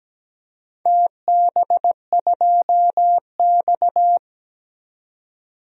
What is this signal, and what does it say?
Morse code, heard as a keyed tone, about 17 wpm, 700 Hz. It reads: TB2X
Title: DXCC-100 Call Signs - 4 Characters - 17wpm